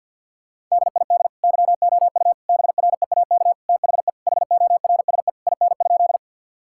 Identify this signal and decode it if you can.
Morse code, heard as a keyed tone, about 50 wpm, 700 Hz. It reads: DID YOU BREAK THE HORSE IN?
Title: Sentences from Top 1000 Words - Review Part 6 - 50wpm